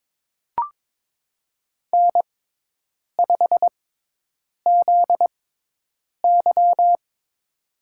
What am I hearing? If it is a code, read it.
Morse code, heard as a keyed tone, about 22 wpm, spaced as for 8 wpm, 700 Hz. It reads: N5ZY
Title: USA Call Signs - 4 Characters - 6x Character Spacing - 22wpm